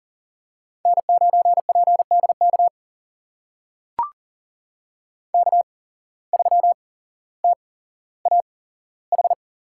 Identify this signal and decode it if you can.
Morse code, heard as a keyed tone, about 39 wpm, spaced as for 17 wpm, 700 Hz. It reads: N9PDK K3TAH
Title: USA Call Signs - 5 Characters - 8x Character Spacing - 40wpm